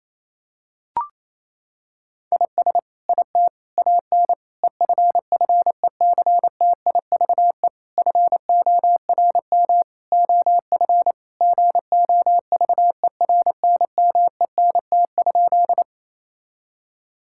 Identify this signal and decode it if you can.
Morse code, heard as a keyed tone, about 28 wpm, 700 Hz. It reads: IS IT AN EFFECTIVE FORM OF GOVERNMENT?
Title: Sentences from Top 1000 Words - Review Part 6 - 28wpm